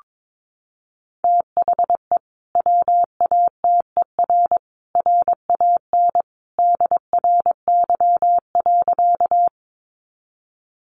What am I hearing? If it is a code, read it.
Morse code, heard as a keyed tone, about 22 wpm, 700 Hz. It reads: THE WATER RAN DRY.
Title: Sentences from Top 500 Words 22wpm